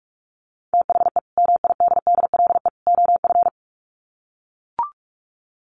Morse code, heard as a keyed tone, about 44 wpm, 700 Hz. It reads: THE MIDDLE OF
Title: Sets of 3 Words 45wpm